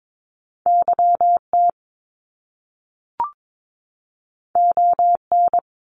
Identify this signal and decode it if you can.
Morse code, heard as a keyed tone, about 22 wpm, 700 Hz. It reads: YT ON
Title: Canadian Provinces and Territories 22wpm